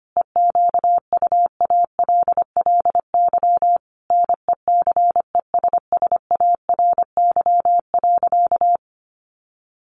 Morse code, heard as a keyed tone, about 25 wpm, 700 Hz. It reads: EQUALLY NECESSARY.
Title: Sentences from Top 1400 Words 25wpm